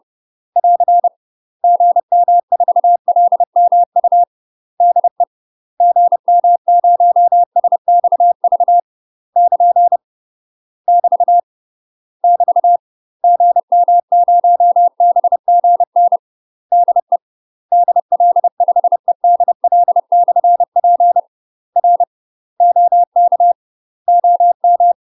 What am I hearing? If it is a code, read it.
Morse code, heard as a keyed tone, about 30 wpm, 700 Hz. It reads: <AR> GM4LMU DE GM0SXV <KN> <BT> <BT> GM0BGN DE DL5EDL/P R OK OM
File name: UK MASTER QSOs TRAINING / 30WPM / UK MASTER QSOs 0039